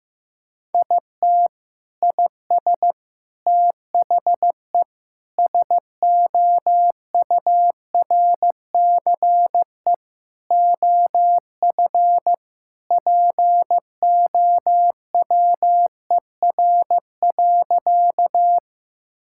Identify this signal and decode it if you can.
Morse code, heard as a keyed tone, about 15 wpm, 700 Hz. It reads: IT IS THE SOURCE OF POWER.
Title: Sentences from Top 1000 Words - Review Part 5 - 15wpm